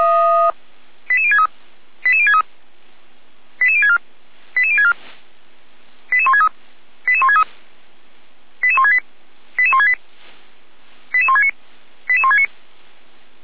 Five-tone selective call (ZVEI), heard as quick runs of five tones, tones of about 70 ms. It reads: 88063 88063 88064 88064 88163 88163 88168 88168 88169 88169